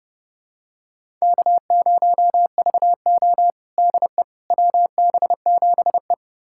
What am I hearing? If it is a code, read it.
Morse code, heard as a keyed tone, about 31 wpm, 700 Hz. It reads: K0VO DE WB7E
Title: USA Callsign DE Callsign - 4 Characters - 30wpm